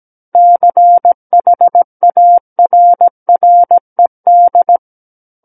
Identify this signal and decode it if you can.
Morse code, heard as a keyed tone, about 17 wpm, 700 Hz. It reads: CHARRED
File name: Book One / 08-Chapter 8 - 17 WPM